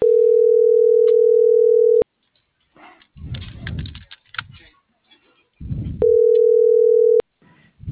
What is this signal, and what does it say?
Ringback tone (North America), 440 + 480 Hz.